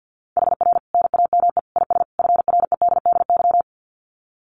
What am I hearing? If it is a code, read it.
Morse code, heard as a keyed tone, about 50 wpm, 700 Hz. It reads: HR NAME IS FREDDY
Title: Common QSO Phrases 50wpm